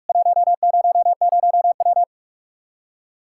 Morse code, heard as a keyed tone, about 46 wpm, 700 Hz. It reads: 100W
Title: Practice QSO - Exchange 2 - Info - 45wpm